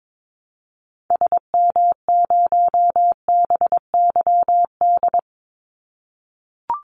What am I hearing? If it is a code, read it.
Morse code, heard as a keyed tone, about 22 wpm, 700 Hz. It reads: SM0BYD
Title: Call Signs - 22wpm